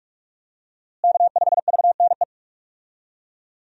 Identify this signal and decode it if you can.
Morse code, heard as a keyed tone, about 46 wpm, 700 Hz. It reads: K5VNE
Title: USA Call Signs - 5 Characters - Repeated 2x - 45wpm